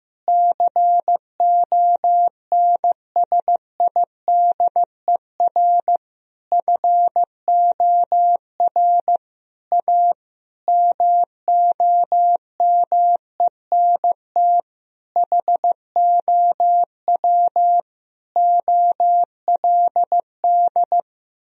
Morse code, heard as a keyed tone, about 15 wpm, 700 Hz. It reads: CONSIDER FOR A MOMENT HOW OLD
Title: Sentences from Top 1000 Words - Review Part 4 - 15wpm